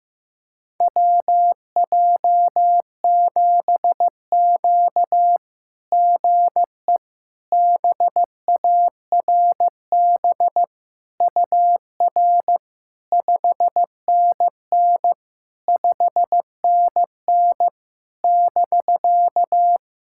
Morse code, heard as a keyed tone, about 15 wpm, 700 Hz. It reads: WJ7Q GE BARB UR 5NN 5NN <BK>
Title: SOTA QSO Practice 15wpm